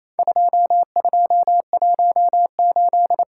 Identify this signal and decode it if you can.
Morse code, heard as a keyed tone, about 28 wpm, 700 Hz. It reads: 2218
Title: Worked All Europe DX Contest - QTC Exchange - 28wpm